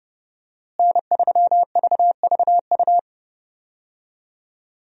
Morse code, heard as a keyed tone, about 32 wpm, 700 Hz. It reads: N3VVU
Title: USA Call Signs - 5 Characters - Repeated 3x - 30wpm